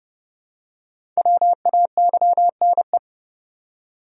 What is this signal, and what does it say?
Morse code, heard as a keyed tone, about 31 wpm, 700 Hz. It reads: WAYNE
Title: USA Common Names - Max 5 Letters - 30wpm